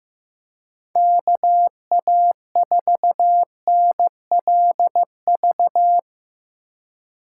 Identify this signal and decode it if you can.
Morse code, heard as a keyed tone, about 17 wpm, 700 Hz. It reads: KA4NLV
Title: Speed Racing - USA 2x3 Call Signs 15wpm